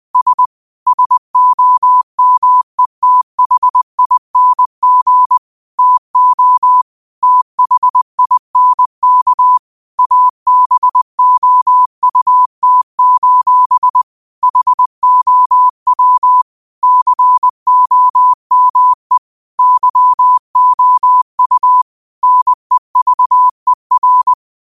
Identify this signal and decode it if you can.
Morse code, heard as a keyed tone, about 20 wpm, 1000 Hz. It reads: S SOMETHING TO THINK ABOUT: HOW COME YOU NEVER
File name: qotdfiles / 20wpm